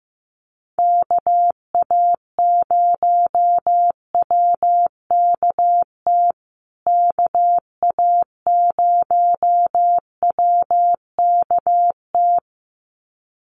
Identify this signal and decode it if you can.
Morse code, heard as a keyed tone, about 15 wpm, 700 Hz. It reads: KA0WKT KA0WKT
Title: USA Call Signs Repeated 2x 15wpm